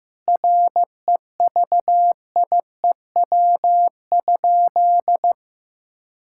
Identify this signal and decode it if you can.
Morse code, heard as a keyed tone, about 15 wpm, 700 Hz. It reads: REVIEW?
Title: Sentences from Top 1000 Words - Review Part 6 - 15wpm